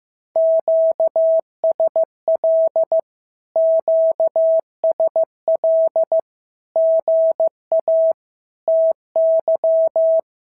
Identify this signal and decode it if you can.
Morse code, heard as a keyed tone, about 14 wpm, 650 Hz. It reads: QSL QSL GA TY
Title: SOTA QSO Practice 15wpm